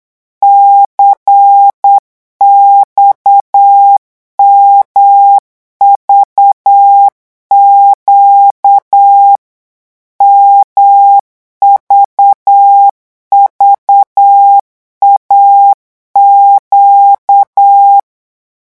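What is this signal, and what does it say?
Morse code, heard as a keyed tone, about 8 wpm, 800 Hz. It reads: CXMVQ MVVAQ
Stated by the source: Morse code revision lesson at 8 words per minute, from a French amateur radio course.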